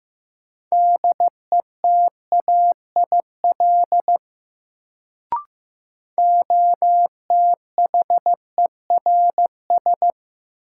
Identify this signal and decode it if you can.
Morse code, heard as a keyed tone, about 15 wpm, 700 Hz. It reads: DETAIL OTHERS
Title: Sentences from Top 1500 Words - New Word Drill Alt - 15wpm